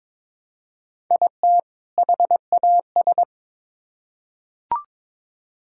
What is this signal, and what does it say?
Morse code, heard as a keyed tone, about 21 wpm, 700 Hz. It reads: IT HAS
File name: Build A Phrase - 2-Word Phrases - 22wpm